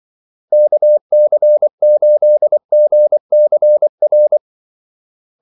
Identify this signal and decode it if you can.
Morse code, heard as a keyed tone, about 24 wpm, 600 Hz. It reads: KC8GCR
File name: Callsigns-24wpm